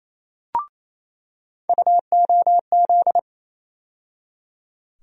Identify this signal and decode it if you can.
Morse code, heard as a keyed tone, about 28 wpm, 700 Hz. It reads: UOZ
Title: USA Call Sign Postfixes - 3 Letters 28wpm